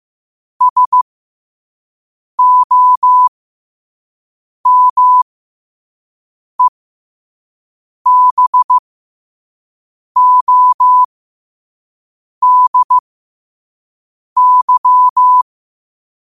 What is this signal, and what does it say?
Morse code, heard as a keyed tone, about 15 wpm, spaced as for 5 wpm, 1000 Hz. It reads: SOMEBODY
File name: 2026-03-16-5wpm